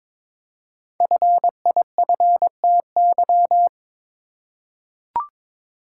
Morse code, heard as a keyed tone, about 22 wpm, 700 Hz. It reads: FIFTY
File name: 5 Letter Words - First 3 Letters Given - 22wpm